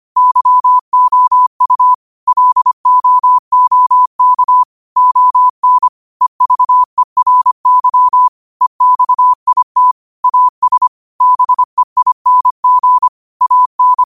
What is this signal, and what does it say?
Morse code, heard as a keyed tone, about 25 wpm, 1000 Hz. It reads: YOU LOOK ON EVERY EXIT AS BEING AN
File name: qotdfiles / 25wpm